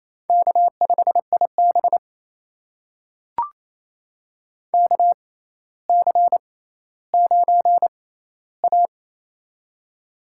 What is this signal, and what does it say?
Morse code, heard as a keyed tone, about 28 wpm, spaced as for 14 wpm, 700 Hz. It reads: K5IB KC9A
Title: USA Call Signs - 4 Characters - 6x Character Spacing - 28wpm